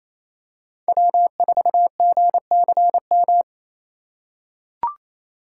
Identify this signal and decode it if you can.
Morse code, heard as a keyed tone, about 28 wpm, 700 Hz. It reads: W4GCM
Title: USA Call Signs - 8x Character Spacing - 28wpm